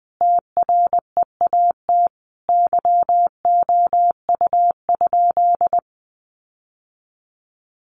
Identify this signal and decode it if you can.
Morse code, heard as a keyed tone, about 20 wpm, 700 Hz. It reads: TREAT YOU?
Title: Sentences from Top 1300 Words 20wpm